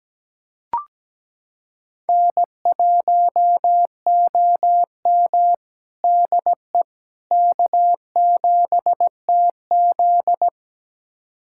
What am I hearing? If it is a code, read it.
Morse code, heard as a keyed tone, about 17 wpm, 700 Hz. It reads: N1OM DE K7TZ
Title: USA Callsign DE Callsign - 4 Characters - 17wpm